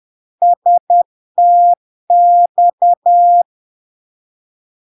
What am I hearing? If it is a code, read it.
Morse code, heard as a keyed tone, about 10 wpm, 700 Hz. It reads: STX
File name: LARK-FD-2022-Log-in-CW-10wpm-700-Hz